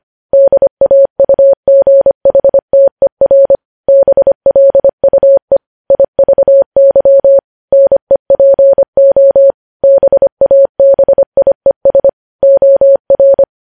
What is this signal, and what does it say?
Morse code, heard as a keyed tone, about 25 wpm, 550 Hz. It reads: DAUGHTER BLUE IVY NEPO BABIES OR
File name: podcast / News headlines-1x-25x25